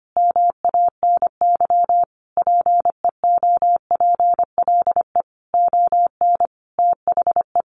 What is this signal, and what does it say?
Morse code, heard as a keyed tone, about 25 wpm, 700 Hz. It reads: MANY PEOPLE ON THE
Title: Sentences from Top 1100 Words 25wpm